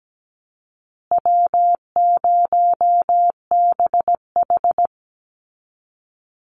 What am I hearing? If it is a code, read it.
Morse code, heard as a keyed tone, about 17 wpm, 700 Hz. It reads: W0BH
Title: Easy Call Signs Repeated 2x - 17wpm